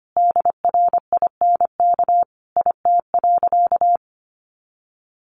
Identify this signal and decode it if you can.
Morse code, heard as a keyed tone, about 25 wpm, 700 Hz. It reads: DRINK IT.
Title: Additional Easy Phrases 25wpm